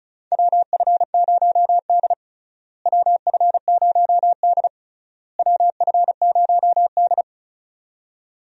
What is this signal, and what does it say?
Morse code, heard as a keyed tone, about 36 wpm, 700 Hz. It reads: WF0D WF0D WF0D
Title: USA Call Signs - 4 Characters - Repeated 3x - 35wpm